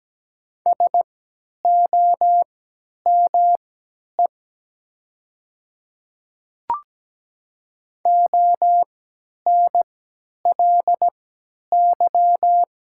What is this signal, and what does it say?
Morse code, heard as a keyed tone, about 17 wpm, spaced as for 10 wpm, 700 Hz. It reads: SOME ONLY
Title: 3x Character Spacing - Top 500 Words - 17wpm